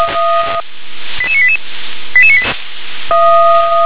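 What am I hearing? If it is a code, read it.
Five-tone selective call (ZVEI), heard as quick runs of five tones, tones of about 70 ms. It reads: 88988
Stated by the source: ZVEI five-tone selective call received on an SDR.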